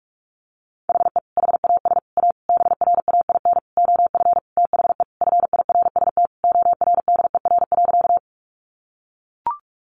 Morse code, heard as a keyed tone, about 44 wpm, 700 Hz. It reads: HE HAS A BRAIN OF THE FIRST ORDER.
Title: Sentences from Top 1100 Words 45wpm